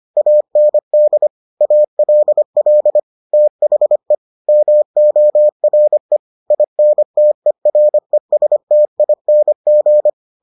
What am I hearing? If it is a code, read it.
Morse code, heard as a keyed tone, about 25 wpm, 600 Hz. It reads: AND ALL THE MORE INTERESTING